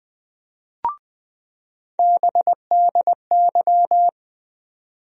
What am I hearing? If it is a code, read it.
Morse code, heard as a keyed tone, about 20 wpm, 700 Hz. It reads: BDY
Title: USA Call Sign Postfixes - 3 Letters 20wpm